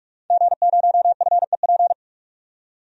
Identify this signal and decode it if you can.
Morse code, heard as a keyed tone, about 43 wpm, 700 Hz. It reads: G0FEP
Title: DXCC-100 Call Signs - 5 Characters - 45wpm